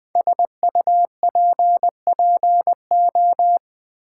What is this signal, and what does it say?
Morse code, heard as a keyed tone, about 20 wpm, 700 Hz. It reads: SUPPO
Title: Build A Word - 5 Letters At A Time - 10-Letter Words - 20wpm